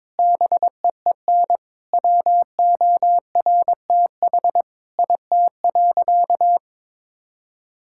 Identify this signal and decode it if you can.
Morse code, heard as a keyed tone, about 22 wpm, 700 Hz. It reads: BEEN WORTH IT.
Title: Sentences from Top 1000 Words - Review Part 5 - 22wpm